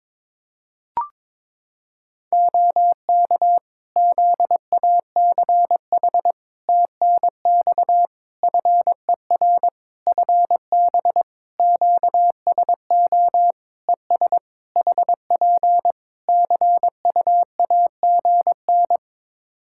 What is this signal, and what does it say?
Morse code, heard as a keyed tone, about 22 wpm, 700 Hz. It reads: OK ZACH TNX FER FB QSO ES HP CUAGN
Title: K1USN SST QSO Practice 22wpm